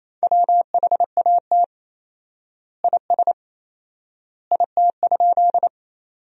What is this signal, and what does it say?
Morse code, heard as a keyed tone, about 28 wpm, 700 Hz. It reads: WHAT IS IT?
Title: 4x Word-Spacing - Sentences from Top 100 Words - 28wpm